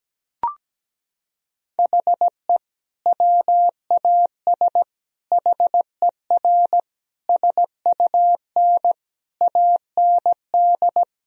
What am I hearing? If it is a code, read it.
Morse code, heard as a keyed tone, about 17 wpm, 700 Hz. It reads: HE WAS HER SUN AND
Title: Sentences from Top 1000 Words - Review Part 3 - 17wpm